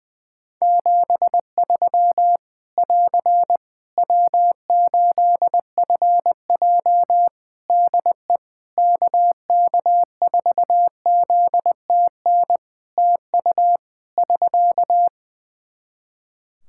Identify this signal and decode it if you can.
Morse code, heard as a keyed tone, about 20 wpm, 700 Hz. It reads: 73 <AR> W8FJ DE KK4ZTN TU <SK>
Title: Practice QSO - Exchange 4 - Ending - 20wpm